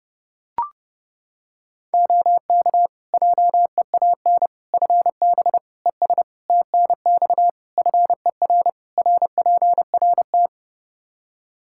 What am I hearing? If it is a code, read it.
Morse code, heard as a keyed tone, about 30 wpm, 700 Hz. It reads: OK JEAN FB ES TNX FER RPRT
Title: Practice QSO - Exchange 2 - Info - 30wpm